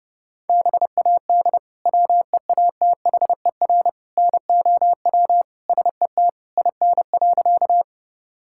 Morse code, heard as a keyed tone, about 30 wpm, 700 Hz. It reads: BAD WEATHER NOW SET IN.
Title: Sentences from Top 1000 Words - Review Part 5 - 30wpm